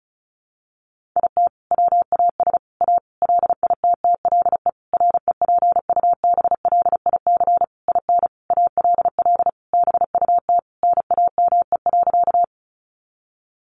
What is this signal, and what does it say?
Morse code, heard as a keyed tone, about 34 wpm, 700 Hz. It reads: IT WAS A LITTLE REPUBLIC IN ALL BUT NAME.
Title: Sentences from Top 1500 Words 35wpm